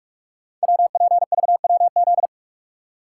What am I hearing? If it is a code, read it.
Morse code, heard as a keyed tone, about 44 wpm, 700 Hz. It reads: WPVWZ
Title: Code Groups - 5 Letters - 45wpm